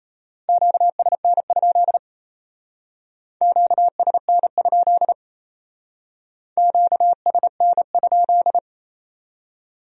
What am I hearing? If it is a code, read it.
Morse code, heard as a keyed tone, about 34 wpm, 700 Hz. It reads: QSN? QSN? QSN?
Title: Speed-Racing Q-Codes - 28wpm